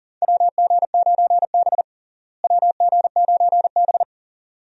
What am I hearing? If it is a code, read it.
Morse code, heard as a keyed tone, about 40 wpm, 700 Hz. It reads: WG9B WG9B
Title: USA Call Signs - 4 Characters - Repeated 2x - 40wpm